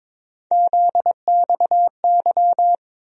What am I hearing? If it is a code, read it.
Morse code, heard as a keyed tone, about 21 wpm, 700 Hz. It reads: ZXY
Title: USA Call Sign Postfixes - 3 Letters 22wpm